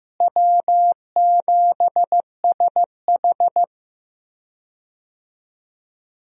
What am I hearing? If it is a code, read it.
Morse code, heard as a keyed tone, about 15 wpm, 700 Hz. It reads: W7SH
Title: USA Call Signs - 4 Characters - Repeated 3x - 15wpm